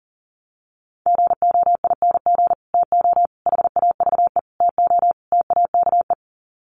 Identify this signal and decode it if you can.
Morse code, heard as a keyed tone, about 40 wpm, 700 Hz. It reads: GOING TO HAVE TO TAKE
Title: Sets of 5 Words Encore 40wpm